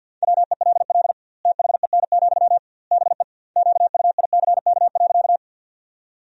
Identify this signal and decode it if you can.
Morse code, heard as a keyed tone, about 51 wpm, 700 Hz. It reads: WELL THEN, BE QUICK.
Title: 2x Word-Spacing - Sentences from Top 500 Words - 50wpm